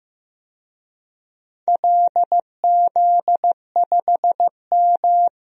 Morse code, heard as a keyed tone, about 15 wpm, 700 Hz. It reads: LZ5M
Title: DXCC-100 Call Signs - 4 Characters - Repeated 3x - 15wpm